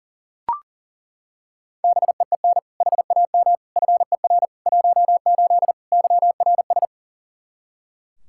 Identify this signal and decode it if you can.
Morse code, heard as a keyed tone, about 40 wpm, 700 Hz. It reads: BEEN HAM FER 18 YRS
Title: Practice QSO - Exchange 3 - More Info - 40wpm